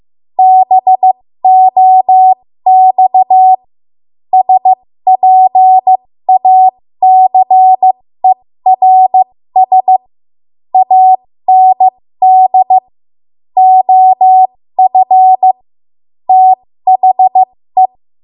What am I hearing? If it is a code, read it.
Morse code, heard as a keyed tone, about 15 wpm, spaced as for 13 wpm, 750 Hz. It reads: BOX SPACERS AND OF THE
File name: Morse / 13WPM